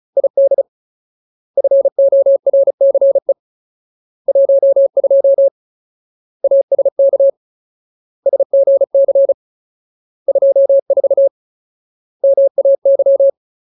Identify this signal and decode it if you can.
Morse code, heard as a keyed tone, about 34 wpm, 550 Hz. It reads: ID FORCE 12 ASK SGC 24 MAY
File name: Ultimate Word List / 35WPM Ultimate